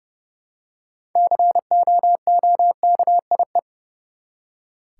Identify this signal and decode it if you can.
Morse code, heard as a keyed tone, about 30 wpm, 700 Hz. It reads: COOKIE
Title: Six Letter Words - 30wpm